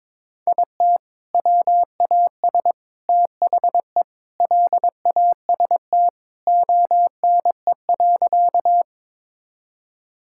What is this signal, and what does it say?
Morse code, heard as a keyed tone, about 22 wpm, 700 Hz. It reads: IT WAS THE LAST ONE.
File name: practice / 22wpm / Sentences with 3-Word Phrases - 22wpm